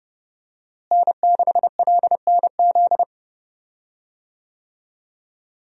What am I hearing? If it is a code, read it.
Morse code, heard as a keyed tone, about 30 wpm, 700 Hz. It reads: N6LNZ
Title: USA Call Signs - 5 Characters - Repeated 2x - 30wpm